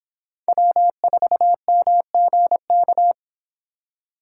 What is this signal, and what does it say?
Morse code, heard as a keyed tone, about 29 wpm, 700 Hz. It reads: W4MGK